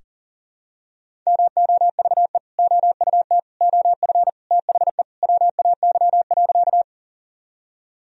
Morse code, heard as a keyed tone, about 40 wpm, 700 Hz. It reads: MOVE OUT OF THE WAY.
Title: Sentences from Top 1000 Words - Review Part 1 - 40wpm